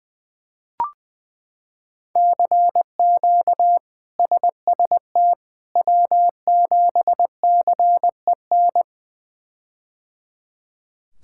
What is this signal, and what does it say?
Morse code, heard as a keyed tone, about 20 wpm, 700 Hz. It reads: CQ SST W7CEN
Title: K1USN SST QSO Practice - 20wpm